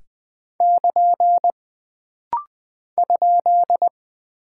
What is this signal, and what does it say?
Morse code, heard as a keyed tone, about 20 wpm, 700 Hz. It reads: <KN> ?
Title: Single Punctuation-Prosign - Rapid-Fire - 20wpm